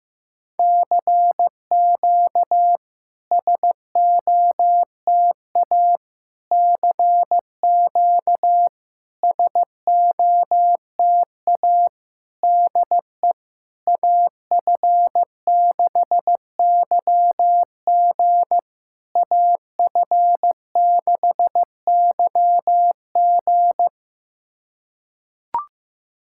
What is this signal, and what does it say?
Morse code, heard as a keyed tone, about 16 wpm, 700 Hz. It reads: CQ SOTA CQ SOTA DE AF6YG AF6YG
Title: SOTA QSO Practice 15wpm